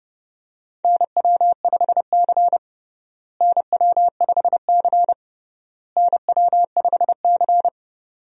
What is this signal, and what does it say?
Morse code, heard as a keyed tone, about 32 wpm, 700 Hz. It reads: NW5C NW5C NW5C
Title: USA Call Signs - 4 Characters - Repeated 3x - 30wpm